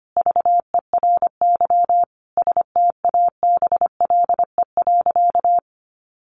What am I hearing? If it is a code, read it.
Morse code, heard as a keyed tone, about 25 wpm, 700 Hz. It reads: VERY STABLE.
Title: Sentences from Top 1500 Words 25wpm